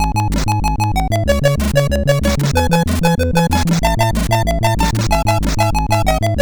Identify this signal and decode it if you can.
There is no code.